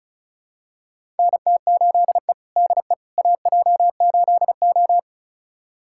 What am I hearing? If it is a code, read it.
Morse code, heard as a keyed tone, about 36 wpm, 700 Hz. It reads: NT8E DE AJ8O
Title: USA Callsign DE Callsign - 4 Characters - 35wpm